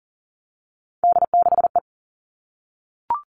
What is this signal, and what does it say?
Morse code, heard as a keyed tone, about 40 wpm, 700 Hz. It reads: D6E
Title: All Cut Numbers - 3-Digit 40wpm